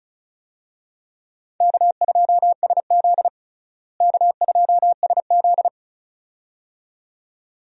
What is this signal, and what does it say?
Morse code, heard as a keyed tone, about 34 wpm, 700 Hz. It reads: K2SZ K2SZ
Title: USA Call Signs - 4 Characters - Repeated 2x - 35wpm